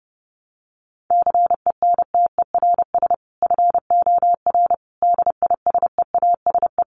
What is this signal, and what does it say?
Morse code, heard as a keyed tone, about 30 wpm, 700 Hz. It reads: CENTERS FOR DISEASE
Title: Sets of 5 Words Encore2 30wpm